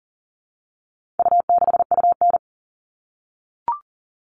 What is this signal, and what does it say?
Morse code, heard as a keyed tone, about 40 wpm, 700 Hz. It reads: U6UN
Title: All Cut Numbers - 4-Digit 40wpm